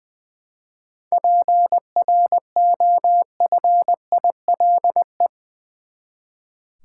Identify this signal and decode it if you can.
Morse code, heard as a keyed tone, about 21 wpm, 700 Hz. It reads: PROFILE
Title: Seven Letter Words - 20wpm